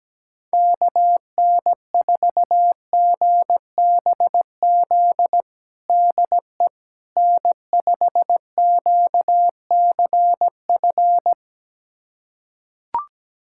Morse code, heard as a keyed tone, about 17 wpm, 700 Hz. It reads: KN4GBZ DE N5QCF
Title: USA Callsign DE Callsign 17wpm